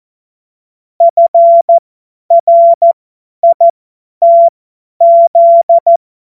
Morse code, heard as a keyed tone, about 14 wpm, spaced as for 10 wpm, 680 Hz. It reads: FRITZ
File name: Morsekurs / Witze / Witz 14. Februar 2023 50 F70